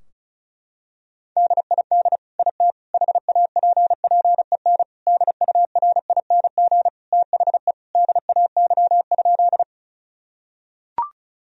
Morse code, heard as a keyed tone, about 36 wpm, 700 Hz. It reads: DID IT HAPPEN DURING THE DAY?
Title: Sentences from Top 1000 Words - Review Part 2 - 35wpm